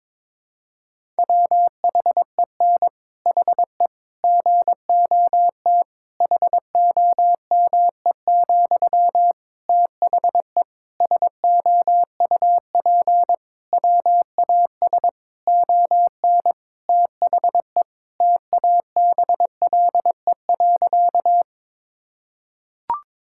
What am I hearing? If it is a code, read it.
Morse code, heard as a keyed tone, about 22 wpm, 700 Hz. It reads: WHEN HE GOT HOME, THE SOUP WAS ON THE TABLE.
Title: Sentences from Top 1000 Words - Review Part 6 - 22wpm